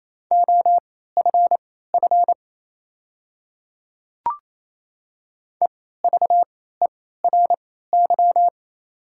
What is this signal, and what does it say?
Morse code, heard as a keyed tone, about 28 wpm, spaced as for 16 wpm, 700 Hz. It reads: OFF EVERY
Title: 3x Character Spacing - Top 200 Words - 28wpm